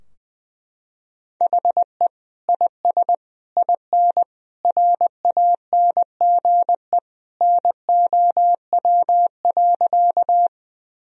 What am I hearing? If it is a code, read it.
Morse code, heard as a keyed tone, about 21 wpm, 700 Hz. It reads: HE IS IN RANGE NOW.
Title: Sentences from Top 1000 Words - Review Part 4 - 20wpm